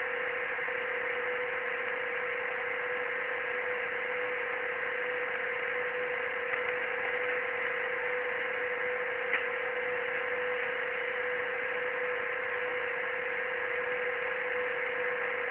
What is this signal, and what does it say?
PSK31, about 31.25 baud.
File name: ARCHIV / BPSK CW / OK1IF-OK1ANG- -1230CET-3766.5-3767